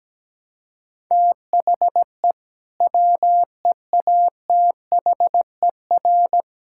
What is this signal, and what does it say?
Morse code, heard as a keyed tone, about 17 wpm, 700 Hz. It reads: THE WEATHER
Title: Sentences from Top 1000 Words - Review Part 5 - 17wpm